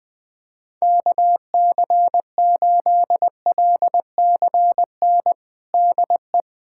Morse code, heard as a keyed tone, about 21 wpm, 700 Hz. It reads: KC8LCN DE
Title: USA Callsign DE Callsign - 6 Characters - 20wpm